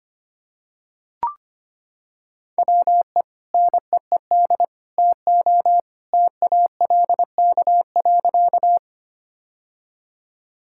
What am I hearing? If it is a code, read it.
Morse code, heard as a keyed tone, about 25 wpm, 700 Hz. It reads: WE NEED TO TALK.
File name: practice / 25wpm / Sentences with 3-Word Phrases - 25wpm